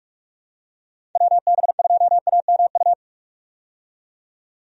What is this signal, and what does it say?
Morse code, heard as a keyed tone, about 43 wpm, 700 Hz. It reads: WB2AMU
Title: USA Call Signs - 6 Characters - 45wpm